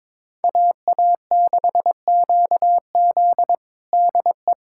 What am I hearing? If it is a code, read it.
Morse code, heard as a keyed tone, about 22 wpm, 700 Hz. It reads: AA6QZ DE
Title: USA Callsign DE Callsign - 5 Characters - 22wpm